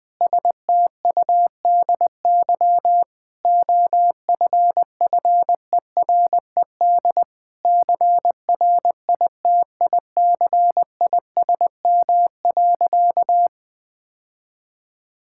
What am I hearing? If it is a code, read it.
Morse code, heard as a keyed tone, about 20 wpm, 700 Hz. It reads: STUDY OFFERED CRITICISM.
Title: Sentences with 8-Word Phrases - 20wpm